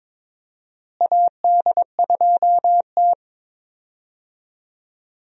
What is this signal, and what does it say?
Morse code, heard as a keyed tone, about 23 wpm, 700 Hz. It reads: AD2T
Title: USA Call Signs - 4 Characters - Repeated 3x - 22wpm